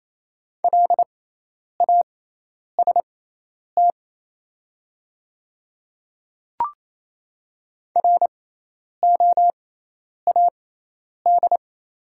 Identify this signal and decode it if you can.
Morse code, heard as a keyed tone, about 28 wpm, spaced as for 10 wpm, 700 Hz. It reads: LAST ROAD
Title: 6x Character Spacing - Top 500 Words - 28wpm